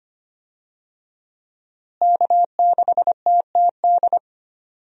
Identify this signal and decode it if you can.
Morse code, heard as a keyed tone, about 25 wpm, 700 Hz. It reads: K6TTD
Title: USA Call Signs - 5 Characters - Repeated 2x - 25wpm